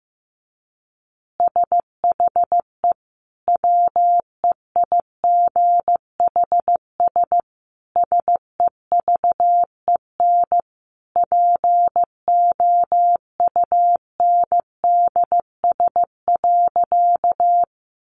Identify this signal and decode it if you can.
Morse code, heard as a keyed tone, about 15 wpm, 700 Hz. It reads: SHE WEIGHS SEVEN POUNDS.
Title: Easy Phrases 15wpm